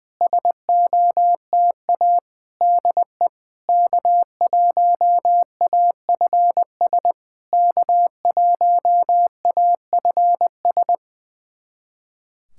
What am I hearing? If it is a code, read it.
Morse code, heard as a keyed tone, about 20 wpm, 700 Hz. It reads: SOTA DE K1AFS K1AFS
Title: SOTA QSO Practice 20wpm